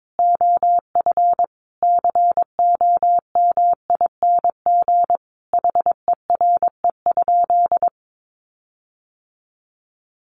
Morse code, heard as a keyed tone, about 22 wpm, 700 Hz. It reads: OF COMING HERE?
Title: Sentences from Top 1300 Words 22wpm